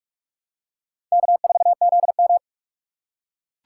Morse code, heard as a keyed tone, about 45 wpm, 690 Hz. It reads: K4ZM
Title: USA Call Signs - 4 Characters - 3x Character Spacing - 45wpm